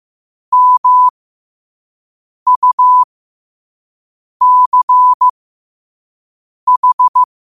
Morse code, heard as a keyed tone, about 15 wpm, spaced as for 5 wpm, 1000 Hz. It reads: MUCH
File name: qotdfiles / 5wpm